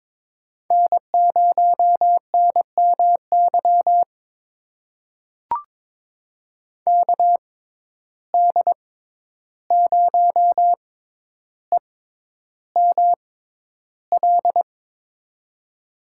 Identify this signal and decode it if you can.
Morse code, heard as a keyed tone, about 22 wpm, spaced as for 11 wpm, 700 Hz. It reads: N0NMY KD0EML
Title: USA Call Signs - 6x Character Spacing - 22wpm